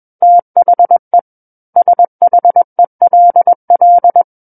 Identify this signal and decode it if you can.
Morse code, heard as a keyed tone, about 21 wpm, 700 Hz. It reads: THE SHELL
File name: Book One / 12-Chapter 12 - 21 WPM